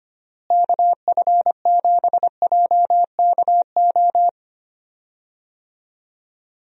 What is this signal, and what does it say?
Morse code, heard as a keyed tone, about 25 wpm, 700 Hz. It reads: KF7JKO
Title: USA Call Signs - 6 Characters - Repeated 2x - 25wpm